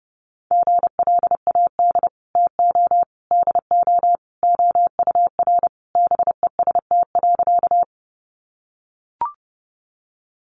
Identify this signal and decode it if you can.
Morse code, heard as a keyed tone, about 30 wpm, 700 Hz. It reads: GLAD TO DO OUR BEST.
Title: Sentences from Top 800 Words 30wpm